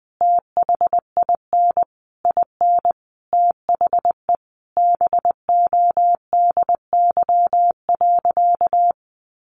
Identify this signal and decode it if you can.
Morse code, heard as a keyed tone, about 20 wpm, 700 Hz. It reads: THIN IN THE BODY.
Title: Sentences from Top 1200 Words 20wpm